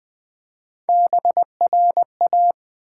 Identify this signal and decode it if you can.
Morse code, heard as a keyed tone, about 21 wpm, 700 Hz. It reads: BRA
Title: Build A Word - 1 Letter At A Time - 5-Letter Words - 20wpm